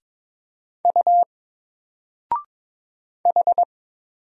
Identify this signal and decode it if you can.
Morse code, heard as a keyed tone, about 22 wpm, 700 Hz. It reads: U H
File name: Single-Letter-Number-22wpm